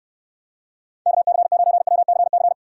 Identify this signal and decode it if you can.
Morse code, heard as a keyed tone, about 64 wpm, 700 Hz. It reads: KC9WCC